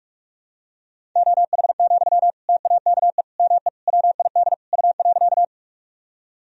Morse code, heard as a keyed tone, about 43 wpm, 700 Hz. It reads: OH, TAKE ME WID U.
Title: Sentences from Top 500 Words and Abbreviations 45wpm